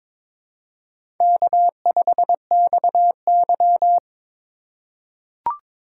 Morse code, heard as a keyed tone, about 22 wpm, 700 Hz. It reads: K5XY
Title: USA Call Signs - 4 Characters - 2x Character Spacing - 22wpm